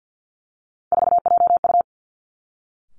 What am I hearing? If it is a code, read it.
Morse code, heard as a keyed tone, about 50 wpm, 700 Hz. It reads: 4JU
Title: Three Challenging Letter-Number Combinations 50wpm